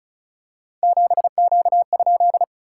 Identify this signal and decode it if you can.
Morse code, heard as a keyed tone, about 36 wpm, 700 Hz. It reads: 7Q?
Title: SOTA QSO Practice 35wpm